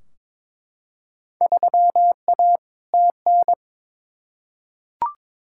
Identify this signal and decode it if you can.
Morse code, heard as a keyed tone, about 23 wpm, 700 Hz. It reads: 3A TN
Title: Field Day Exchanges - 22wpm